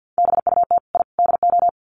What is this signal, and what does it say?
Morse code, heard as a keyed tone, about 50 wpm, 700 Hz. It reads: BUT I DO
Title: Sets of 3 Words 50wpm